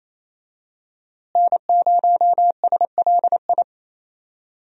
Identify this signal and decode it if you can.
Morse code, heard as a keyed tone, about 27 wpm, 700 Hz. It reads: N0SLI
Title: USA Call Signs - 5 Characters - Repeated 2x - 28wpm